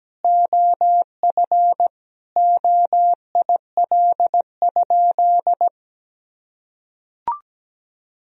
Morse code, heard as a keyed tone, about 17 wpm, 700 Hz. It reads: OF OIL?
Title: Sentences from Top 1000 Words - Review Part 4 - 17wpm